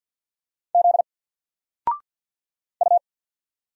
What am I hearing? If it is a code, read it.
Morse code, heard as a keyed tone, about 46 wpm, 700 Hz. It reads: Z U
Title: Single Letter-Number 50wpm